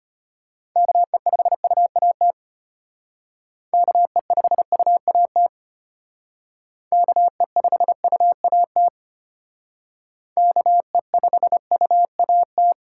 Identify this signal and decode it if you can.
Morse code, heard as a keyed tone, about 32 wpm, 700 Hz. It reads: KE5UAT KE5UAT KE5UAT KE5UAT